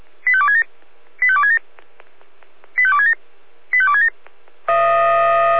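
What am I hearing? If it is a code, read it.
Five-tone selective call (ZVEI), heard as quick runs of five tones, tones of about 70 ms. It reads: 85267 85267 85267 85267
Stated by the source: Five-tone ZVEI selective call received on SDR.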